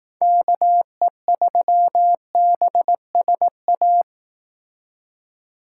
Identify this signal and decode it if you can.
Morse code, heard as a keyed tone, about 18 wpm, 700 Hz. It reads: KE3BSA
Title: Speed Racing - USA 2x3 Call Signs 15wpm